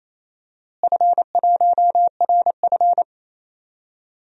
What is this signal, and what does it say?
Morse code, heard as a keyed tone, about 27 wpm, 700 Hz. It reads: F1RF
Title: DXCC-100 Call Signs - 4 Characters - Repeated 3x - 28wpm